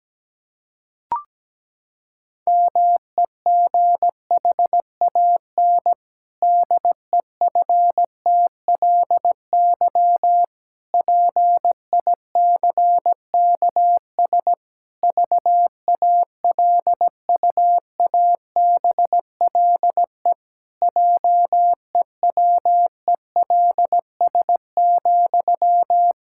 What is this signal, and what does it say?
Morse code, heard as a keyed tone, about 17 wpm, 700 Hz. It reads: MEGHAN DEFTLY PICKS VALUABLE JEWELS,
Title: Pangram Challenge v2 - 17wpm